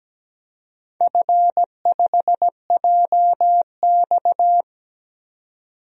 Morse code, heard as a keyed tone, about 17 wpm, 700 Hz. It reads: F5JX
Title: DXCC-100 Call Signs - 4 Characters - 17wpm